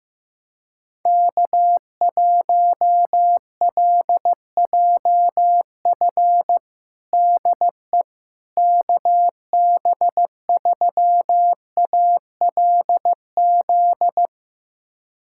Morse code, heard as a keyed tone, about 15 wpm, 700 Hz. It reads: K1LJF DE KB3ALZ
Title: USA Callsign DE Callsign 15wpm